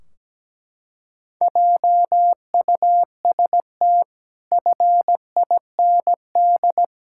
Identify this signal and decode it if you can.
Morse code, heard as a keyed tone, about 17 wpm, 700 Hz. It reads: JUST FIND
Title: Sentences from Top 1600 Words 17wpm